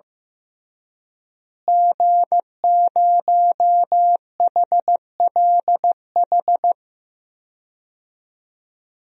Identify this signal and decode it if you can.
Morse code, heard as a keyed tone, about 15 wpm, 700 Hz. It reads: G0HLH
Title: DXCC-100 Call Signs - 5 Characters - Repeated 3x - 15wpm